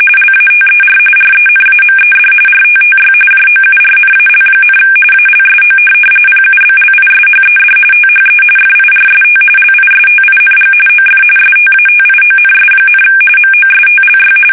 RTTY, about 45.45 baud. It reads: LATER RENAMED RTTY JOURNAL, ALSO PUBLISHED THE FIRST LISTING OF STATIONS, MOSTLY
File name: fsk850 / teletype-RTTY